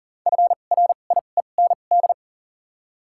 Morse code, heard as a keyed tone, about 41 wpm, spaced as for 29 wpm, 700 Hz. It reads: FRIEND
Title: 2x Character Spacing - Top 500 Words - 40wpm